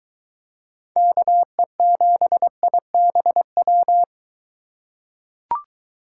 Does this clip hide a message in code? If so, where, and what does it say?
Yes: Morse code from 1.0 to 4.0 s, heard as a keyed tone, about 22 wpm, 700 Hz. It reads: KE7IBW